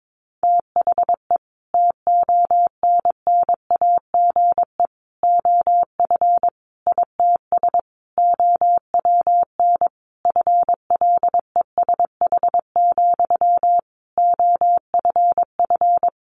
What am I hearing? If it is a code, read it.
Morse code, heard as a keyed tone, about 22 wpm, 700 Hz. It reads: THE TONNAGE OF ITS OWN FLESH, OFF
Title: A Sound Of Thunder 22wpm